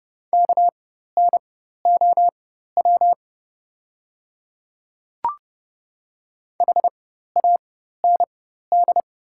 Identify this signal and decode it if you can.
Morse code, heard as a keyed tone, about 30 wpm, spaced as for 14 wpm, 690 Hz. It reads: KNOW HAND
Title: 4x Character Spacing - Top 300 Words - 30wpm